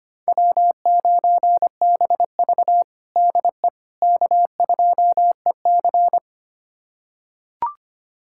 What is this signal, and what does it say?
Morse code, heard as a keyed tone, about 25 wpm, 700 Hz. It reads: W9BV DE K2EC
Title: USA Callsign DE Callsign - 4 Characters - 25wpm